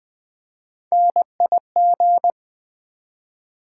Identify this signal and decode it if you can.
Morse code, heard as a keyed tone, about 20 wpm, 700 Hz. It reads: NIG
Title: Build A Word - 1 Letter At A Time - 5-Letter Words - 20wpm